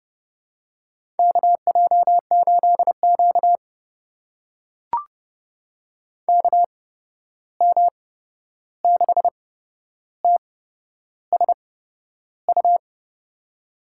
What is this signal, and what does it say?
Morse code, heard as a keyed tone, about 30 wpm, spaced as for 11 wpm, 700 Hz. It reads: KJ8Q KM6TSU
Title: USA Call Signs - 8x Character Spacing - 30wpm